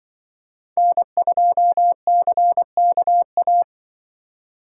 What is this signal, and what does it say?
Morse code, heard as a keyed tone, about 24 wpm, 700 Hz. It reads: N2CKA
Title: Speed Racing - USA Call Signs 20wpm